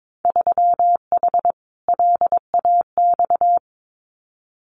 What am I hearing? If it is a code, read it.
Morse code, heard as a keyed tone, about 22 wpm, 700 Hz. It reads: 3H LAX
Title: Winter Field Day Exchanges 22wpm